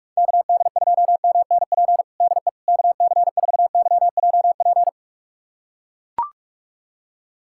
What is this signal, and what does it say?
Morse code, heard as a keyed tone, about 41 wpm, 700 Hz. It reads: KD2MNP DE KC4YJP
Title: USA Callsign DE Callsign 45wpm